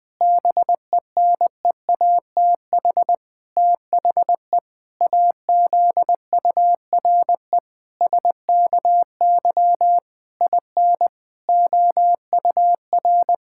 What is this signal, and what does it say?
Morse code, heard as a keyed tone, about 20 wpm, 700 Hz. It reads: BENEATH THE AZURE SKY IN OUR
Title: Pangram Challenge v2 - 20wpm